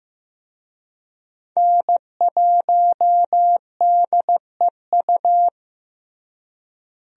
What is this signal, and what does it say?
Morse code, heard as a keyed tone, about 16 wpm, 700 Hz. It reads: N1DEU
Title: USA Call Signs - 5 Characters - Repeated 2x - 15wpm